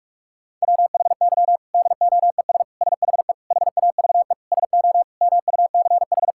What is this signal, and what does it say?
Morse code, heard as a keyed tone, about 46 wpm, 700 Hz. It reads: WHY DOES SHE HAVE SO MUCH
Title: Sentences from Top 1000 Words - Review Part 6 - 45wpm